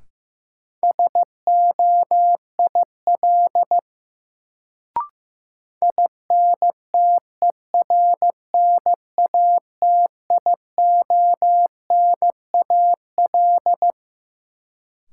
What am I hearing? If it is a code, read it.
Morse code, heard as a keyed tone, about 15 wpm, 700 Hz. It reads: SOIL INTERNATIONAL
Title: Sentences from Top 1000 Words - Review Part 4 - New Word Drill - 15wpm